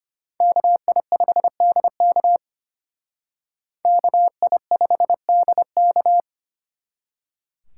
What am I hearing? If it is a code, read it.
Morse code, heard as a keyed tone, about 33 wpm, 700 Hz. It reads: KI5DK KI5DK